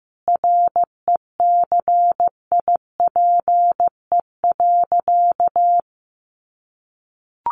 Morse code, heard as a keyed tone, about 15 wpm, 700 Hz. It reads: RECIPE.
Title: Sentences from Top 1200 Words 15wpm